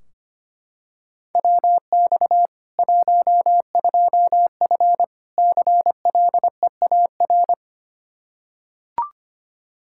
Morse code, heard as a keyed tone, about 25 wpm, 700 Hz. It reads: WX 12F CLEAR
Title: Practice QSO - Exchange 2 - Info - 25wpm